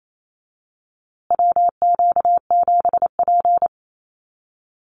Morse code, heard as a keyed tone, about 28 wpm, 700 Hz. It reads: WQ7P
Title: USA Call Signs Repeated 2x 28wpm